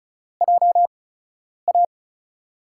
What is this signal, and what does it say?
Morse code, heard as a keyed tone, about 34 wpm, spaced as for 10 wpm, 700 Hz. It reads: JA
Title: USA Call Signs - 6 Characters - 8x Character Spacing - 35wpm